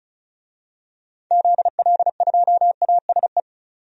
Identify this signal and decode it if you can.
Morse code, heard as a keyed tone, about 36 wpm, 700 Hz. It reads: ZL2ASE
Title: DXCC-100 Call Signs - 6 Characters - 35wpm